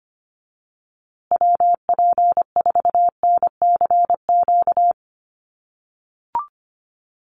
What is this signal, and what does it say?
Morse code, heard as a keyed tone, about 24 wpm, 700 Hz. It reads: WP4NCQ
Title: USA Call Signs Repeated 2x 25wpm